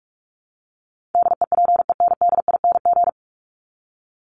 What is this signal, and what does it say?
Morse code, heard as a keyed tone, about 44 wpm, 700 Hz. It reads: DEPENDING
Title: ING Suffix - Top 100 Words with Base Words 45wpm